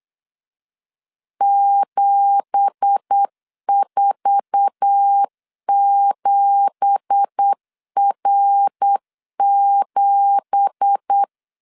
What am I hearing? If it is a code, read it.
Morse code, heard as a keyed tone, about 8 wpm, 800 Hz. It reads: 747R7
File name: 8wpm